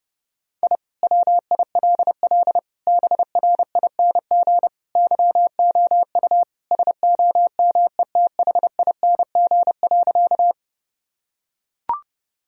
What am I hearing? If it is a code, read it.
Morse code, heard as a keyed tone, about 32 wpm, 700 Hz. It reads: I WILL BRING YOU SOMETHING.
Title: Sentences from Top 1000 Words - Review Part 3 - 30wpm